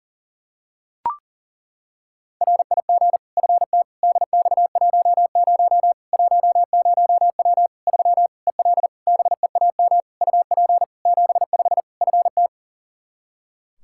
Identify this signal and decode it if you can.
Morse code, heard as a keyed tone, about 37 wpm, 700 Hz. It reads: RIG FT DX10 10W 3 EL BEAM UP 75 FT
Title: K1USN SST QSO Practice 40wpm